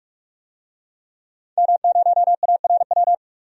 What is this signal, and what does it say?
Morse code, heard as a keyed tone, about 46 wpm, 700 Hz. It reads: M0ARW
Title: DXCC-100 Call Signs - 5 Characters - 45wpm